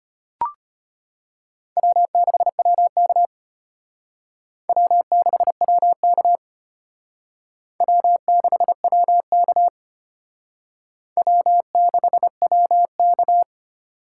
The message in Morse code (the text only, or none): W6WK W6WK W6WK W6WK